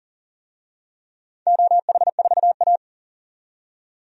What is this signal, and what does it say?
Morse code, heard as a keyed tone, about 40 wpm, 700 Hz. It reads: OH4A
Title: DXCC-100 Call Signs - 4 Characters - 40wpm